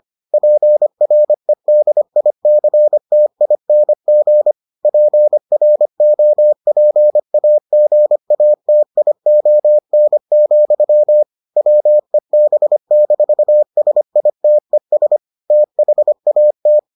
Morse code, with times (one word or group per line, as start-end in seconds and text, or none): 0.3-4.5 PREDICTING
4.8-11.2 PROPAGATION,
11.6-15.2 WEB-SITES
15.5-16.8 THAT